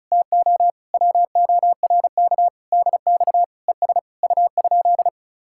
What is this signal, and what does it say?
Morse code, heard as a keyed tone, about 44 wpm, 700 Hz. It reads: TO WORK DX ES U?